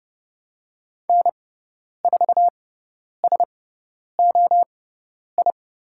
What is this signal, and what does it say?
Morse code, heard as a keyed tone, about 29 wpm, spaced as for 10 wpm, 700 Hz. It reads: N4SOI
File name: Lesson 18g - Letter m - Call Signs 30wpm